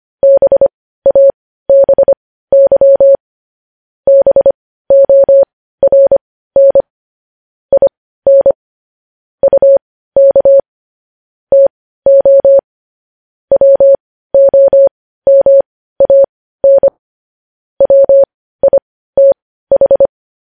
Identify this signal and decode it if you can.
Morse code, heard as a keyed tone, about 25 wpm, spaced as for 15 wpm, 550 Hz. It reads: BABY BORN IN UK TO WOMAN WITH
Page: news-headlines-in-morse-code-at-15-wpm